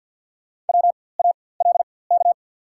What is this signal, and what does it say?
Morse code, heard as a keyed tone, about 52 wpm, spaced as for 23 wpm, 690 Hz. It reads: WALK
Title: 4x Character Spacing - Top 300 Words - 50wpm